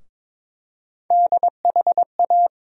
Morse code, heard as a keyed tone, about 21 wpm, 700 Hz. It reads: DHA
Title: USA Call Sign Postfixes - 3 Letters 22wpm